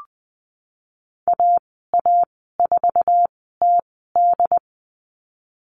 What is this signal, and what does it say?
Morse code, heard as a keyed tone, about 20 wpm, spaced as for 15 wpm, 700 Hz. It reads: AA4TD
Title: 147 Lesson 24g - Letter y - Call Signs 20wpm